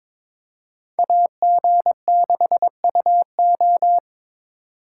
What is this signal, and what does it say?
Morse code, heard as a keyed tone, about 26 wpm, 700 Hz. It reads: AG6UO
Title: Speed Racing - USA 2x2 Call Signs 22wpm